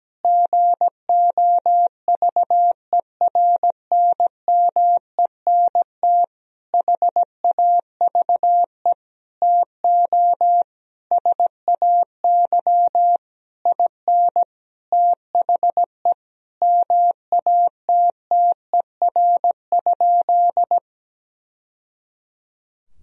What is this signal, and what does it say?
Morse code, heard as a keyed tone, about 17 wpm, 700 Hz. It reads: GOVERNMENT HAVE TO SAY IN THE MATTER?
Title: Sentences from Top 1000 Words - Review Part 3 - 17wpm